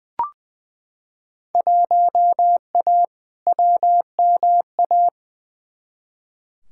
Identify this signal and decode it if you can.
Morse code, heard as a keyed tone, about 20 wpm, 700 Hz. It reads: 1A WMA
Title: Field Day Exchanges - 20wpm